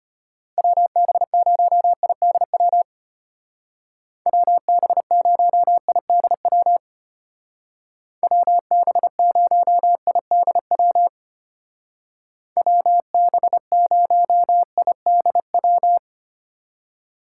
Morse code, heard as a keyed tone, about 30 wpm, 700 Hz. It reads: WB0IDW WB0IDW WB0IDW WB0IDW
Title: Speed Racing - USA 2x3 Call Signs 25wpm